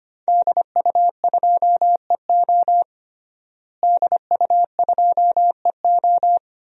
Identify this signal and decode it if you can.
Morse code, heard as a keyed tone, about 25 wpm, 700 Hz. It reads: DU2EO DU2EO
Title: DXCC-100 Call Signs - 5 Characters - Repeated 3x - 25wpm